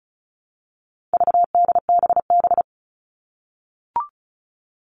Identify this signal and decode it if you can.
Morse code, heard as a keyed tone, about 34 wpm, 700 Hz. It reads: VDBB
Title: All Cut Numbers - 4-Digit 35wpm